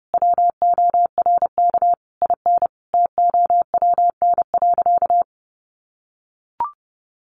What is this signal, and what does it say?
Morse code, heard as a keyed tone, about 30 wpm, 700 Hz. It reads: WORK IN TOWN.
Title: Sentences from Top 1500 Words 30wpm